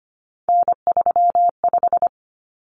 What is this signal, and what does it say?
Morse code, heard as a keyed tone, about 25 wpm, 700 Hz. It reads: N35
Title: Common Cut Numbers - 3-Digit 25wpm